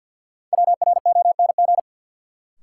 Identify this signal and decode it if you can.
Morse code, heard as a keyed tone, about 51 wpm, 700 Hz. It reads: WRONG
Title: Sentences from Top 900 Words - New Word Drill - 50wpm